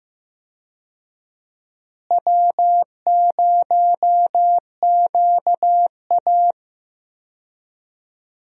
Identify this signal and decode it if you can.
Morse code, heard as a keyed tone, about 15 wpm, 700 Hz. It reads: W0QA
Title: USA Call Signs - 4 Characters - Repeated 3x - 15wpm